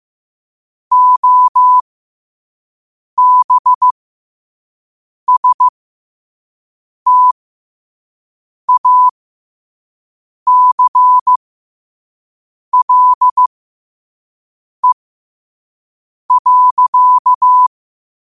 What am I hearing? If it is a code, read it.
Morse code, heard as a keyed tone, about 15 wpm, spaced as for 5 wpm, 1000 Hz. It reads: OBSTACLE.
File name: qotdfiles / 5wpm